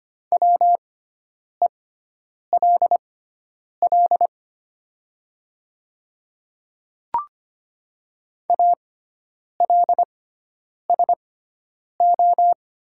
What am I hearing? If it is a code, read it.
Morse code, heard as a keyed tone, about 25 wpm, spaced as for 9 wpm, 690 Hz. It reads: WELL ALSO
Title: 6x Character Spacing - Top 200 Words - 25wpm